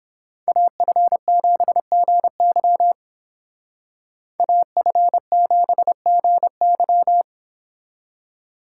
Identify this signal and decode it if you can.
Morse code, heard as a keyed tone, about 30 wpm, 700 Hz. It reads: AF7GY AF7GY